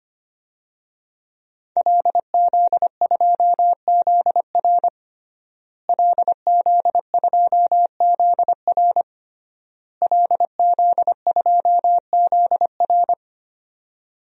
Morse code, heard as a keyed tone, about 25 wpm, 700 Hz. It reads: LZ2ZR LZ2ZR LZ2ZR
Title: DXCC-100 Call Signs - 5 Characters - Repeated 3x - 25wpm